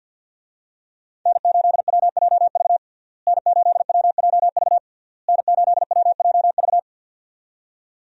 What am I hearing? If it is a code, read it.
Morse code, heard as a keyed tone, about 52 wpm, 700 Hz. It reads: N8WJV N8WJV N8WJV
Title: USA Call Signs - 5 Characters - Repeated 3x - 50wpm